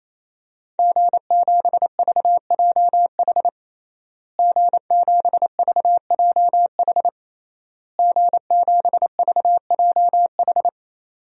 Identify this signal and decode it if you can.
Morse code, heard as a keyed tone, about 28 wpm, 690 Hz. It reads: G7VJH G7VJH G7VJH
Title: DXCC-100 Call Signs - 5 Characters - Repeated 3x - 28wpm